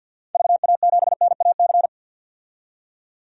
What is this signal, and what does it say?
Morse code, heard as a keyed tone, about 52 wpm, 700 Hz. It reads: VA7NAC
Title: DXCC-100 Call Signs - 6 Characters - 50wpm